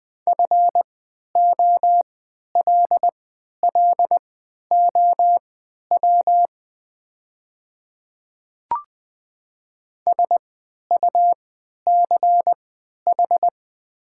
Morse code, heard as a keyed tone, about 20 wpm, spaced as for 11 wpm, 700 Hz. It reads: FOLLOW SUCH
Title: 3x Character Spacing - Top 300 Words - 20wpm